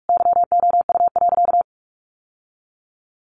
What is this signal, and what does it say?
Morse code, heard as a keyed tone, about 44 wpm, 700 Hz. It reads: YOU.
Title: Sentences from Top 1200 Words 45wpm